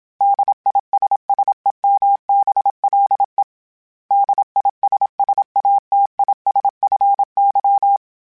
Morse code, heard as a keyed tone, about 26 wpm, 800 Hz. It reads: DISSEMBLE DISSATISFY
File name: General / General-mp3s / ISS @25